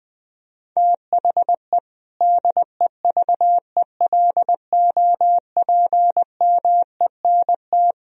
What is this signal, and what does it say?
Morse code, heard as a keyed tone, about 20 wpm, 700 Hz. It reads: THE DEVELOPMENT
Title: Sentences from Top 1000 Words - Review Part 4 - 20wpm